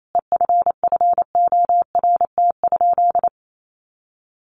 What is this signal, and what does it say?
Morse code, heard as a keyed tone, about 28 wpm, 700 Hz. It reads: EFFORT?
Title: Sentences from Top 1400 Words 28wpm